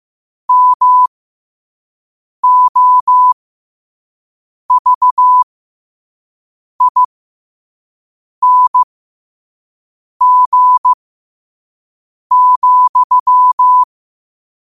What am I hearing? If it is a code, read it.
Morse code, heard as a keyed tone, about 15 wpm, spaced as for 5 wpm, 1000 Hz. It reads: MOVING,
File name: qotdfiles / 5wpm